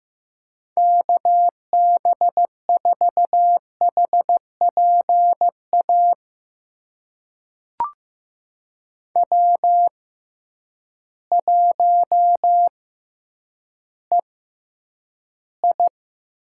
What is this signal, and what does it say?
Morse code, heard as a keyed tone, about 15 wpm, spaced as for 9 wpm, 700 Hz. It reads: KB4HPA W1EI
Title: USA Call Signs - 6x Character Spacing - 15wpm